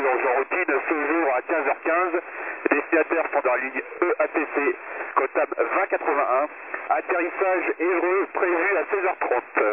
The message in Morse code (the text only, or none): none